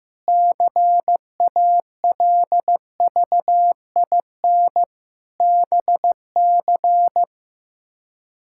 Morse code, heard as a keyed tone, about 15 wpm, 700 Hz. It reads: CALVIN BC
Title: K1USN SST QSO Practice - 15wpm